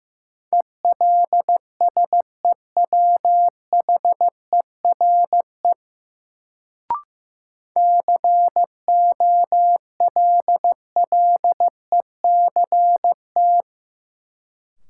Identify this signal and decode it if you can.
Morse code, heard as a keyed tone, about 15 wpm, 700 Hz. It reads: ELSEWHERE COLLECT
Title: Sentences from Top 1600 Words - New Word Drill Alt - 15wpm